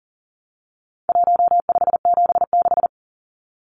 Morse code, heard as a keyed tone, about 40 wpm, 700 Hz. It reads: 1576
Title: Numbers - 4-Digit 40wpm